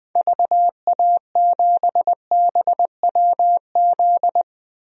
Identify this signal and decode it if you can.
Morse code, heard as a keyed tone, about 21 wpm, 700 Hz. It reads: VA7BWZ
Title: DXCC-100 Call Signs - 6 Characters - Repeated 2x - 20wpm